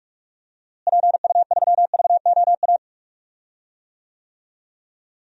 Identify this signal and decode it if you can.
Morse code, heard as a keyed tone, about 46 wpm, 700 Hz. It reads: PU3VOA
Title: DXCC-100 Call Signs - 6 Characters - Repeated 3x - 45wpm